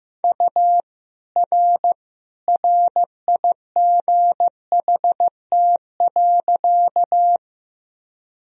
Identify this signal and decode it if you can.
Morse code, heard as a keyed tone, about 15 wpm, 700 Hz. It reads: U R RIGHT.
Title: Sentences from Top 200 Words and Abbreviations 15wpm